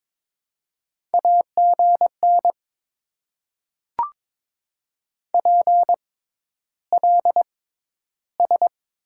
Morse code, heard as a keyed tone, about 23 wpm, spaced as for 11 wpm, 700 Hz. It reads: AGN PLS
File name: Top 90 QSO Elements - 6x Character Spacing - 22wpm